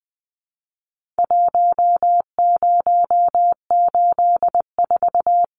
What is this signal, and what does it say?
Morse code, heard as a keyed tone, about 20 wpm, 700 Hz. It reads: 1084
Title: Numbers - Contest Style - 4-Digit 20wpm